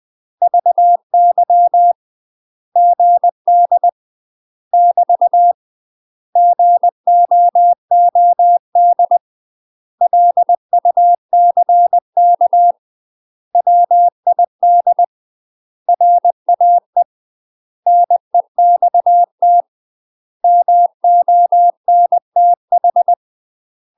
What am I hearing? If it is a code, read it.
Morse code, heard as a keyed tone, about 20 wpm, 700 Hz. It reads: VY GD <BT> GOOD LUCK WID RAE NEXT MONTH
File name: UK MASTER QSOs TRAINING/20WPM